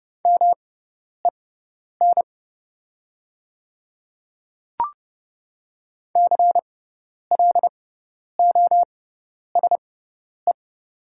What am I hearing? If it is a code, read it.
Morse code, heard as a keyed tone, about 31 wpm, spaced as for 10 wpm, 700 Hz. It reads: MEN CLOSE
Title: 6x Character Spacing - Top 300 Words - 30wpm